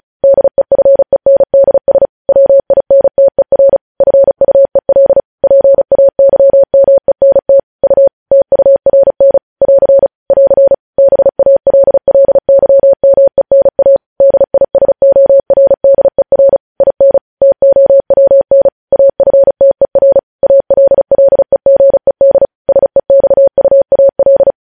Morse code, heard as a keyed tone, about 36 wpm, 550 Hz. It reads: DEFENDS WINTER FUEL PAYMENT U TURN <AR> <AR> BALLYMENA DISORDER IN TOWN AFTER ALLEGED SEXUAL
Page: news headlines in morse code at 35 wpm